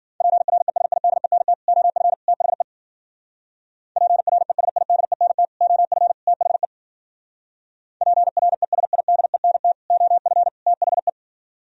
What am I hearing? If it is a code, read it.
Morse code, heard as a keyed tone, about 50 wpm, 700 Hz. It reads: PRESIDENT OF THE PRESIDENT OF THE PRESIDENT OF THE